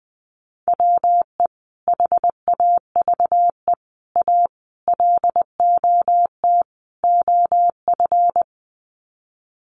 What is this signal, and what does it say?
Morse code, heard as a keyed tone, about 20 wpm, 700 Hz. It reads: WE HAVE A LOT OF
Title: Sets of 5 Words 20wpm